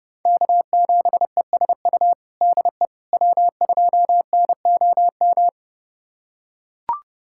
Morse code, heard as a keyed tone, about 30 wpm, 700 Hz. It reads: K7ESU DE W2NOM
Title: USA Callsign DE Callsign - 5 Characters - 30wpm